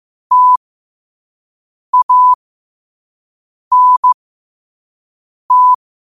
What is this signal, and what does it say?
Morse code, heard as a keyed tone, about 15 wpm, spaced as for 5 wpm, 1000 Hz. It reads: TANT
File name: qotdfiles / 5wpm